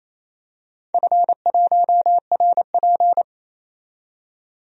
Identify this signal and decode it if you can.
Morse code, heard as a keyed tone, about 29 wpm, 700 Hz. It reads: F1RP
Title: DXCC-100 Call Signs - 4 Characters - 28wpm